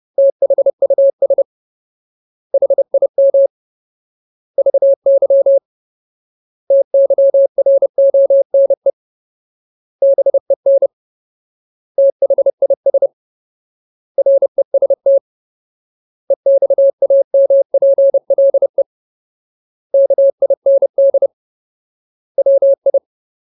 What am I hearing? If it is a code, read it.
Morse code, heard as a keyed tone, about 30 wpm, 550 Hz. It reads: THUS HIM VY TYRONE BEN THIS REST EXAMPLE KIND WI
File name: Ultimate Word List / 30WPM Ultimate